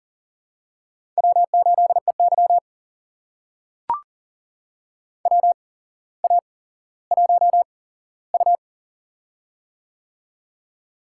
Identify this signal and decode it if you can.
Morse code, heard as a keyed tone, about 40 wpm, spaced as for 17 wpm, 700 Hz. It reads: W8EY WA1U
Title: USA Call Signs - 4 Characters - 8x Character Spacing - 40wpm